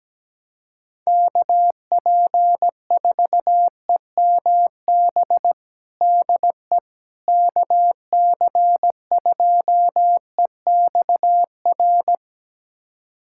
Morse code, heard as a keyed tone, about 17 wpm, 700 Hz. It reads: KP4EMB DE KC2EXR
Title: USA Callsign DE Callsign 17wpm